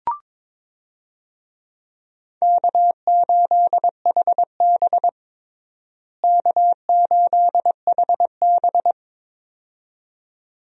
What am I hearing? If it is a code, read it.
Morse code, heard as a keyed tone, about 22 wpm, 700 Hz. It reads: K8HB K8HB
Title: USA Call Signs - 4 Characters - Repeated 2x - 22wpm